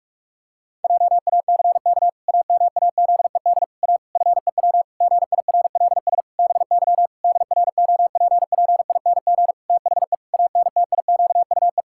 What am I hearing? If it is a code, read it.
Morse code, heard as a keyed tone, about 46 wpm, 700 Hz. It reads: JACK AMAZED A FEW GIRLS BY DROPPING THE ANTIQUE
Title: Pangram Challenge v2 - 45wpm